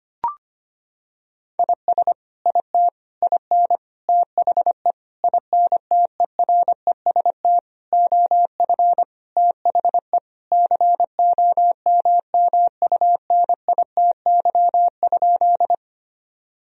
Morse code, heard as a keyed tone, about 25 wpm, 700 Hz. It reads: IS IT IN THE INTEREST OF THE COMMUNITY?
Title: Sentences from Top 1000 Words - Review Part 4 - 25wpm